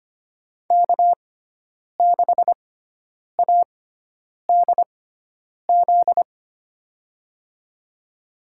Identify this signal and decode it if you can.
Morse code, heard as a keyed tone, about 25 wpm, spaced as for 9 wpm, 700 Hz. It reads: K6ADZ
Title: USA Call Signs - 5 Characters - 6x Character Spacing - 25wpm